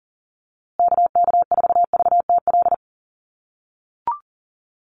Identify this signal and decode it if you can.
Morse code, heard as a keyed tone, about 40 wpm, 700 Hz. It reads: KK4VTL
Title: USA Call Signs Repeated 2x 40wpm